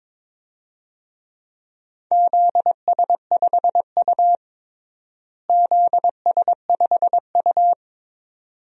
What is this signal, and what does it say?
Morse code, heard as a keyed tone, about 22 wpm, 700 Hz. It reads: ZS5U ZS5U
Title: DXCC-100 Call Signs - 4 Characters - Repeated 2x - 22wpm